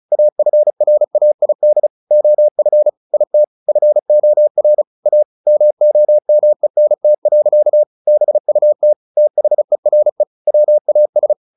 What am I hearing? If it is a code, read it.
Morse code, heard as a keyed tone, about 34 wpm, 600 Hz. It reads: AFRAID OF IT FOR A MOMENT. BUT THERE WAS
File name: the old man and the sea 2 35wpm